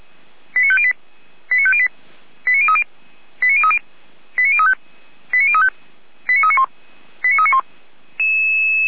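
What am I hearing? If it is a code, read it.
Five-tone selective call (ZVEI), heard as quick runs of five tones, tones of about 70 ms. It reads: 89508 89508 89030 89030 89035 89035 89391 89391
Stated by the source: ZVEI five-tone selective call received on SDR, 455.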